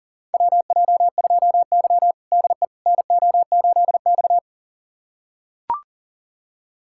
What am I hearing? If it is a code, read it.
Morse code, heard as a keyed tone, about 37 wpm, 700 Hz. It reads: WJ2Y DE NO8X
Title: USA Callsign DE Callsign - 4 Characters - 40wpm